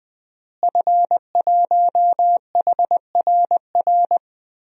Morse code, heard as a keyed tone, about 21 wpm, 700 Hz. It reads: F1HRR
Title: DXCC-100 Call Signs - 5 Characters - Repeated 3x - 20wpm